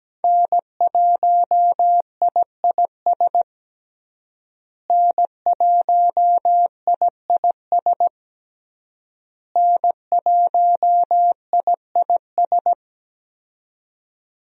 Morse code, heard as a keyed tone, about 17 wpm, 700 Hz. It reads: N1IIS N1IIS N1IIS
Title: USA Call Signs - 5 Characters - Repeated 3x - 17wpm